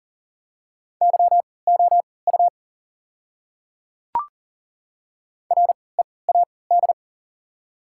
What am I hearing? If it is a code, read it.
Morse code, heard as a keyed tone, about 39 wpm, spaced as for 23 wpm, 690 Hz. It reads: YOU READ
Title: 3x Character Spacing - Top 200 Words - 40wpm